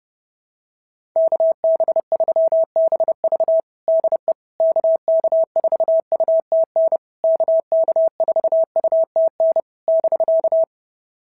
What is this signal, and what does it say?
Morse code, heard as a keyed tone, about 30 wpm, 650 Hz. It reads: KB3BV DE KK4UTN KK4UTN <BK>
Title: SOTA QSO Practice 30wpm